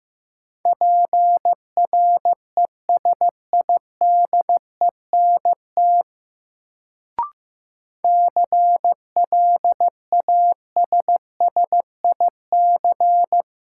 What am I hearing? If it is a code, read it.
Morse code, heard as a keyed tone, about 15 wpm, 700 Hz. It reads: PRESIDENT CLASSIC
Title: Sentences from Top 1000 Words - New Word Drill - 15wpm